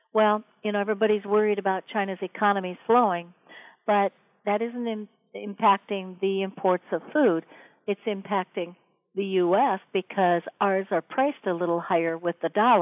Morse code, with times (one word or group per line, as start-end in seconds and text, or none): none